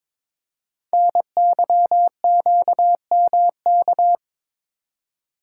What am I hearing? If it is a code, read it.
Morse code, heard as a keyed tone, about 22 wpm, 700 Hz. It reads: NYQMK
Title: Code Groups - 5 Letters - 22wpm